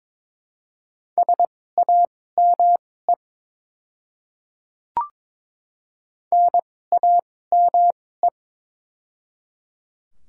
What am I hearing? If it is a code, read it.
Morse code, heard as a keyed tone, about 22 wpm, spaced as for 16 wpm, 700 Hz. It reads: SAME NAME
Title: 2x Character Spacing - Top 200 Words - 22wpm